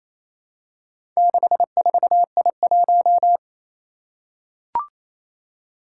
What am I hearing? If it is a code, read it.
Morse code, heard as a keyed tone, about 28 wpm, 700 Hz. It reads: 64I1
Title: Four Challenging Letter-Number Combinations 28wpm